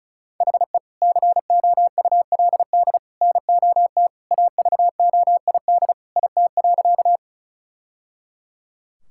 Morse code, heard as a keyed tone, about 36 wpm, 700 Hz. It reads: HE COULD NOT AVOID IT.
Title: Sentences from Top 1000 Words - Review Part 6 - 35wpm